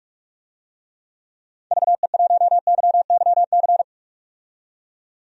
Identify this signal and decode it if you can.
Morse code, heard as a keyed tone, about 42 wpm, 700 Hz. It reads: VE1YYC
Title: DXCC-100 Call Signs - 6 Characters - 45wpm